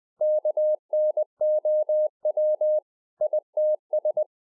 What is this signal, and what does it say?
Morse code, heard as a keyed tone, about 20 wpm, 610 Hz. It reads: KNOW ITS
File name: 2011-04-mp3 / A SMALL TRANCEIVER 3 20WPM